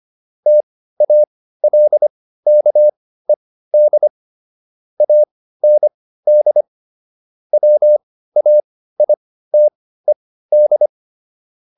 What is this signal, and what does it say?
Morse code, heard as a keyed tone, about 25 wpm, spaced as for 15 wpm, 600 Hz. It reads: TALKED AND WAITED